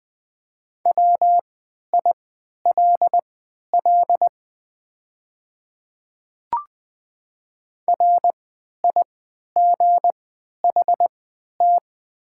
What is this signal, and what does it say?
Morse code, heard as a keyed tone, about 20 wpm, spaced as for 11 wpm, 690 Hz. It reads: WILL RIGHT
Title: 3x Character Spacing - Top 200 Words - 20wpm